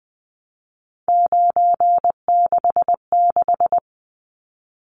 Morse code, heard as a keyed tone, about 20 wpm, 700 Hz. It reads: 966
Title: Numbers - 3-Digit 20wpm